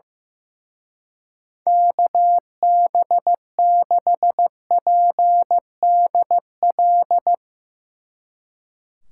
Morse code, heard as a keyed tone, about 16 wpm, 700 Hz. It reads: KB6PDL
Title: Speed Racing - USA Call Signs 15wpm